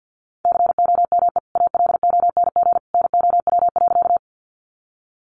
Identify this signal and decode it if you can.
Morse code, heard as a keyed tone, about 50 wpm, 700 Hz. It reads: COME ALONG NOW.
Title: Sentences from Top 700 Words 50wpm